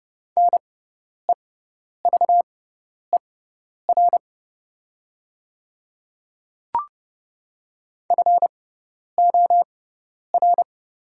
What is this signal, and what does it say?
Morse code, heard as a keyed tone, about 30 wpm, spaced as for 10 wpm, 700 Hz. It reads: NEVER FOR
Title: 6x Character Spacing - Top 300 Words - 30wpm